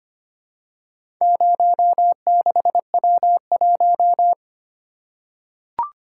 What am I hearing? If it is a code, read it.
Morse code, heard as a keyed tone, about 25 wpm, 700 Hz. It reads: 06W1
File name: practice / 25wpm / Four Letter-Number Combinations - 25wpm